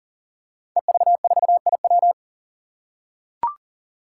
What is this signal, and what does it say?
Morse code, heard as a keyed tone, about 40 wpm, 700 Hz. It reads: EV4IW
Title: Five Challenging Letter-Number Combinations 40wpm